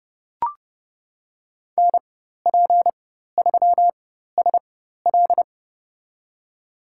Morse code, heard as a keyed tone, about 30 wpm, spaced as for 14 wpm, 700 Hz. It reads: NP3SL
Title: USA Call Signs - 5 Characters - 4x Character Spacing - 30wpm